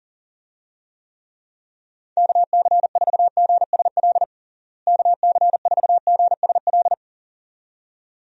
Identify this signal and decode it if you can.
Morse code, heard as a keyed tone, about 40 wpm, 700 Hz. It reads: KC4GSL KC4GSL
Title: USA Call Signs - 6 Characters - Repeated 2x - 40wpm